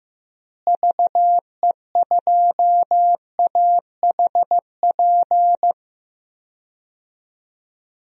Morse code, heard as a keyed tone, about 15 wpm, 700 Hz. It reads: VE2AHP
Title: DXCC-100 Call Signs - 6 Characters - Repeated 2x - 15wpm